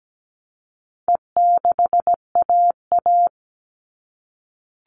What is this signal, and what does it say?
Morse code, heard as a keyed tone, about 17 wpm, 700 Hz. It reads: E6AA
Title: All Cut Numbers - 4-Digit 17wpm